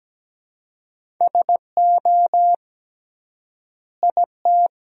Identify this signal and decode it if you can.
Morse code, heard as a keyed tone, about 17 wpm, 700 Hz. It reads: SO IT
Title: 3x Word-Spacing - Sentences from Top 100 Words - 17wpm